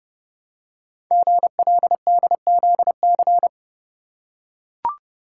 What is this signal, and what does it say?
Morse code, heard as a keyed tone, about 29 wpm, 700 Hz. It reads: GLDZC
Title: Code Groups - 5 Letters - 30wpm